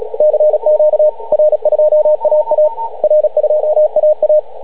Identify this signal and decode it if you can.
Morse code, heard as a keyed tone, about 36 wpm, 620 Hz, with radio noise. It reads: CQ R2AA R2AA